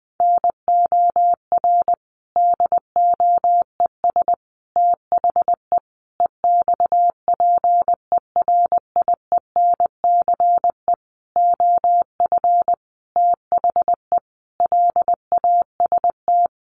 Morse code, heard as a keyed tone, about 20 wpm, 700 Hz. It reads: NOR DOES THE EXPERIENCE OF THE LAST
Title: Sentences from Top 1500 Words 20wpm